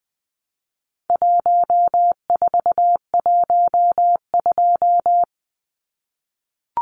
Morse code, heard as a keyed tone, about 20 wpm, 700 Hz. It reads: 1412
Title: Numbers - 4-Digit 20wpm